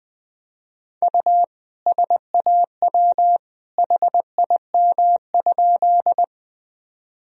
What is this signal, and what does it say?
Morse code, heard as a keyed tone, about 20 wpm, 700 Hz. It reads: U SAW HIM?
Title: Sentences from Top 300 Words and Abbreviations 20wpm